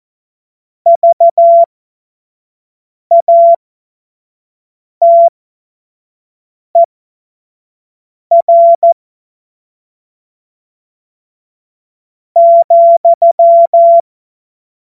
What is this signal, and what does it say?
Morse code, heard as a keyed tone, about 14 wpm, spaced as for 5 wpm, 680 Hz. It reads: VATER ,
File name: Morsekurs / Witze / Witz 21. März 2023 25 F70